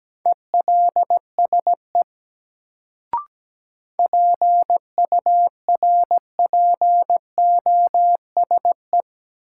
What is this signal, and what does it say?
Morse code, heard as a keyed tone, about 17 wpm, 700 Hz. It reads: ELSE PURPOSE
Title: Sentences from Top 800 Words - New Word Drill - 17wpm